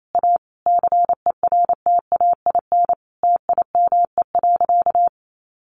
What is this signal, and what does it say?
Morse code, heard as a keyed tone, about 28 wpm, 700 Hz. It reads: A CERTAIN TIME.
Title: Sentences from Top 1000 Words 28wpm